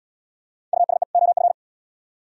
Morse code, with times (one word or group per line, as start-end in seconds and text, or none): 0.7-1.0 USE
1.1-1.5 OF